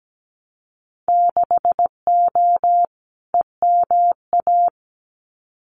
Morse code, heard as a keyed tone, about 17 wpm, 700 Hz. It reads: 6O EMA
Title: Winter Field Day Exchanges Encore 17wpm